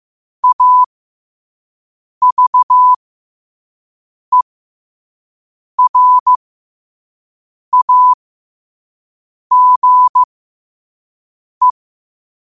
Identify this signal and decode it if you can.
Morse code, heard as a keyed tone, about 15 wpm, spaced as for 5 wpm, 1000 Hz. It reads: AVERAGE
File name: qotdfiles / 5wpm